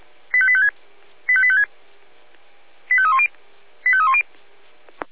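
Five-tone selective call (ZVEI), heard as quick runs of five tones, tones of about 70 ms. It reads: 85856 85856 85310 85310
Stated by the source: Received by SDR as ZVEI five-tone selective call 85259.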